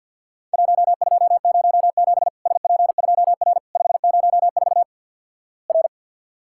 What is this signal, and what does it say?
Morse code, heard as a keyed tone, about 50 wpm, 700 Hz. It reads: 1207 SP2R 504 R
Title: Worked All Europe DX Contest - QTC Exchange - 50wpm